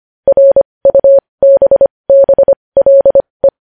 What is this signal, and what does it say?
Morse code, heard as a keyed tone, about 25 wpm, spaced as for 20 wpm, 550 Hz. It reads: RUBBLE
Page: news headlines in morse code at 20 wpm